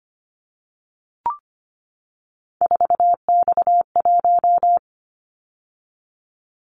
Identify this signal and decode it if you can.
Morse code, heard as a keyed tone, about 25 wpm, 700 Hz. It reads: 4X1
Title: Callsign Prefixes for Top 50 DXCC - 25wpm